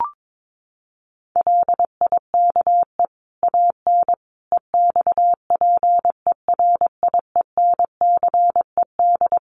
Morse code, heard as a keyed tone, about 22 wpm, 700 Hz. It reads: LIKE AN EXPERIENCED
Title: Sentences from Top 1500 Words 22wpm